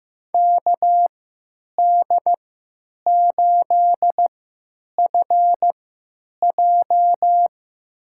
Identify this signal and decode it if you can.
Morse code, heard as a keyed tone, about 15 wpm, spaced as for 8 wpm, 700 Hz. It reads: KD8FJ
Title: USA Call Signs - 5 Characters - 3x Character Spacing - 15wpm